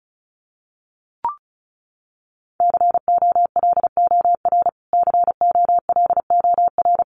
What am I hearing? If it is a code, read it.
Morse code, heard as a keyed tone, about 34 wpm, 700 Hz. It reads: COLOR COLOR
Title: Top 500 Words Repeated 35wpm